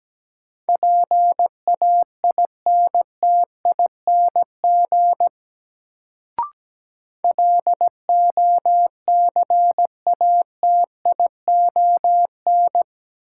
Morse code, heard as a keyed tone, about 17 wpm, 700 Hz. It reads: PAINTING LOCATION
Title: Sentences from Top 1000 Words - Review Part 6 - New Word Drill - 17wpm